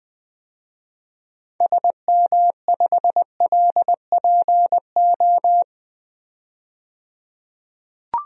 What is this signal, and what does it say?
Morse code, heard as a keyed tone, about 20 wpm, 700 Hz. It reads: SM5LPO
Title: DXCC-100 Call Signs - 6 Characters - 20wpm